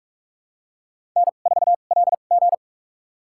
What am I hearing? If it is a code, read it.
Morse code, heard as a keyed tone, about 46 wpm, spaced as for 32 wpm, 700 Hz. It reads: N4LG
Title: USA Call Signs - 2x Character Spacing - 45wpm